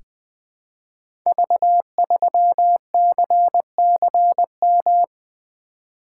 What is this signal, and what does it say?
Morse code, heard as a keyed tone, about 20 wpm, 700 Hz. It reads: V3CCM
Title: Code Groups - 5 Letter:Numbers - 20wpm